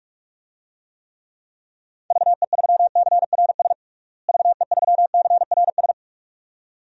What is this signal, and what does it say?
Morse code, heard as a keyed tone, about 46 wpm, 700 Hz. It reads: VE3CRS VE3CRS
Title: DXCC-100 Call Signs - 6 Characters - Repeated 2x - 45wpm